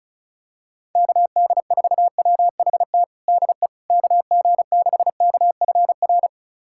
Morse code, heard as a keyed tone, about 34 wpm, 700 Hz. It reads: KD4WHT DE KG6KFR
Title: USA Callsign DE Callsign - 6 Characters - 35wpm